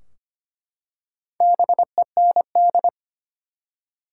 Morse code, heard as a keyed tone, about 25 wpm, 700 Hz. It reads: BEND
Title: Four Letter Words - 25wpm